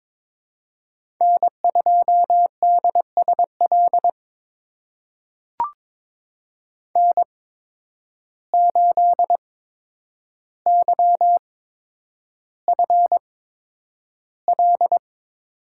Morse code, heard as a keyed tone, about 22 wpm, spaced as for 9 wpm, 700 Hz. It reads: N2DSL N8YFL
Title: USA Call Signs - 8x Character Spacing - 22wpm